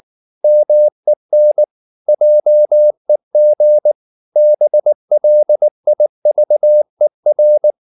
Morse code, heard as a keyed tone, about 19 wpm, 600 Hz. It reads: MEN JEG BLIVER
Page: Ferie 19 wpm